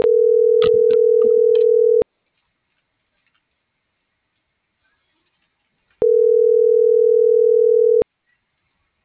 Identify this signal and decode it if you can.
Ringback tone (North America), 440 + 480 Hz.